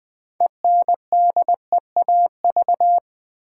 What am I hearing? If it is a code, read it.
Morse code, heard as a keyed tone, about 20 wpm, 700 Hz. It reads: ENDEAV
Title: Build A Word - 2 Letters At A Time - 8-Letter Words - 20wpm